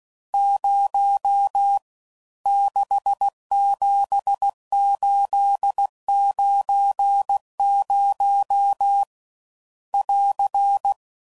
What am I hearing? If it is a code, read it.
Morse code, heard as a keyed tone, about 16 wpm, 800 Hz. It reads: 0 67890 <AR>
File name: Mixed / Misc / Numbs